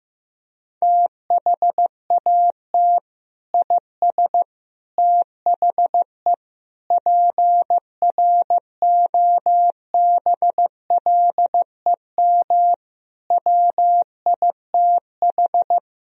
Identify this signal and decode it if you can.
Morse code, heard as a keyed tone, about 15 wpm, 700 Hz. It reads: THAT IS THE PROBLEM WITH
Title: Sentences from Top 1000 Words - Review Part 2 - 15wpm